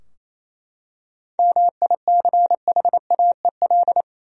Morse code, heard as a keyed tone, about 28 wpm, 700 Hz. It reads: MICHAEL
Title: USA Common Names - 28wpm